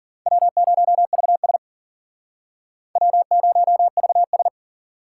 Morse code, heard as a keyed tone, about 53 wpm, 700 Hz. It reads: W0VS W0VS